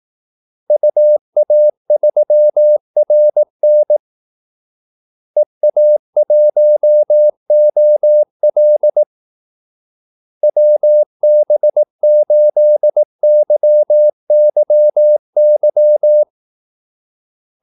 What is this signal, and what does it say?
Morse code, heard as a keyed tone, about 18 wpm, 600 Hz. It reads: UA3RN EA1OL WB8YYY
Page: Kaldesignaler 18 wpm